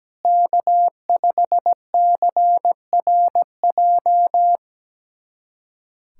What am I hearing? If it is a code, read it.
Morse code, heard as a keyed tone, about 17 wpm, 700 Hz. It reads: K5CRJ
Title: USA Callsign DE Callsign - 5 Characters - 17wpm